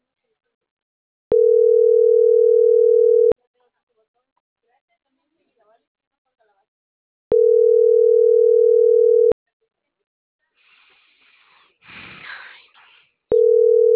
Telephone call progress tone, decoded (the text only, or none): ringback tone